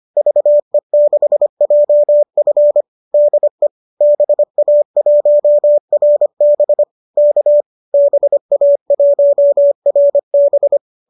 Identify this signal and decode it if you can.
Morse code, heard as a keyed tone, about 25 wpm, 590 Hz. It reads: VE6JF DE BA1RB K BA1RB
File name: qso203 25